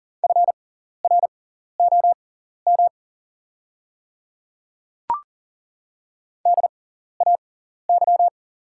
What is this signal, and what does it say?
Morse code, heard as a keyed tone, about 40 wpm, spaced as for 14 wpm, 700 Hz. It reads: FROM DAY
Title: 6x Character Spacing - Top 100 Words - 40wpm